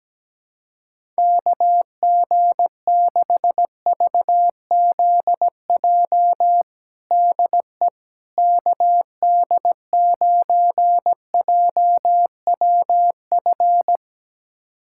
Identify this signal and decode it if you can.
Morse code, heard as a keyed tone, about 17 wpm, 700 Hz. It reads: KG6VZJ DE KD9JWF
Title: USA Callsign DE Callsign - 6 Characters - 17wpm